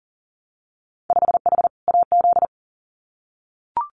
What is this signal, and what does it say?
Morse code, heard as a keyed tone, about 38 wpm, 700 Hz. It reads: 5H AZ
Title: Winter Field Day Exchanges Encore 40wpm